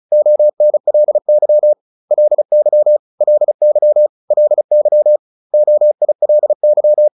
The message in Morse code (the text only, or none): ONLY LY LY LY OILY